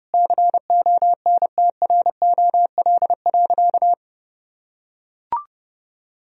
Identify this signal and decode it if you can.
Morse code, heard as a keyed tone, about 30 wpm, 700 Hz. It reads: CONTROL.
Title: Sentences from Top 1000 Words - Review Part 4 - 30wpm